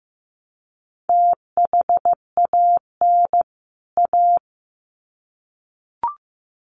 Morse code, heard as a keyed tone, about 15 wpm, 700 Hz. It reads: THAN A
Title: Sets of 2 Words 15wpm